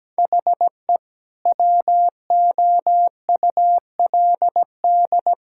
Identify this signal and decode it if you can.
Morse code, heard as a keyed tone, about 17 wpm, 700 Hz. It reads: HE WOULD
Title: Sentences from Top 1000 Words - Review Part 5 - 17wpm